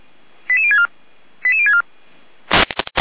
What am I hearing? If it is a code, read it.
Five-tone selective call (ZVEI), heard as quick runs of five tones, tones of about 70 ms. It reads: 88064 88064
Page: SDR/ZVEI